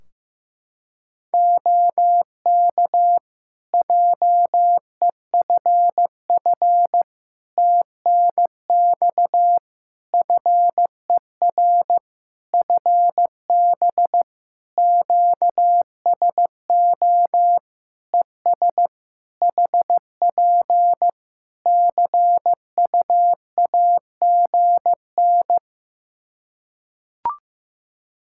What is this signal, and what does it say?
Morse code, heard as a keyed tone, about 15 wpm, 700 Hz. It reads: OK JEFF TNX FER FB QSO ES HP CUAGN
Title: K1USN SST QSO Practice 15wpm